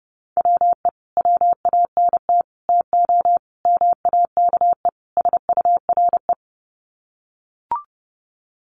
Morse code, heard as a keyed tone, about 30 wpm, 700 Hz. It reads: WE WANT TO MAKE SURE
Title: Sets of 5 Words Encore2 30wpm